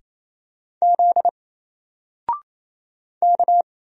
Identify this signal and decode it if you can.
Morse code, heard as a keyed tone, about 27 wpm, 700 Hz. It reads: Z K
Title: Single Character (Letter:Number:Punctuation:Prosign) Alt 28wpm